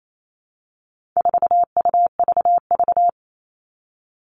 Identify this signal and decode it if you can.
Morse code, heard as a keyed tone, about 28 wpm, 700 Hz. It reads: 4UVV
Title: All Cut Numbers - 4-Digit 28wpm